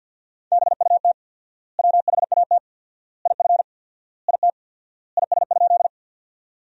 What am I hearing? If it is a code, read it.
Morse code, heard as a keyed tone, about 50 wpm, 700 Hz. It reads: BUT WHAT IF IT IS?
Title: 4x Word-Spacing - Sentences from Top 100 Words - 50wpm